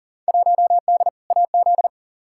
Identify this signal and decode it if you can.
Morse code, heard as a keyed tone, about 40 wpm, 700 Hz. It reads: 1D AZ
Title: Field Day Exchanges - 40wpm